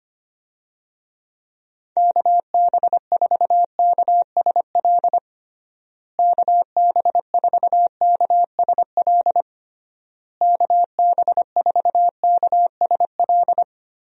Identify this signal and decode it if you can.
Morse code, heard as a keyed tone, about 25 wpm, 700 Hz. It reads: KB4KSL KB4KSL KB4KSL
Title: USA Call Signs - 6 Characters - Repeated 3x - 25wpm